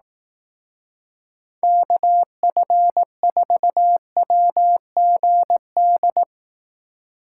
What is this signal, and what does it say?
Morse code, heard as a keyed tone, about 18 wpm, 700 Hz. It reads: KF4WGD
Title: Speed Racing - USA 2x3 Call Signs 15wpm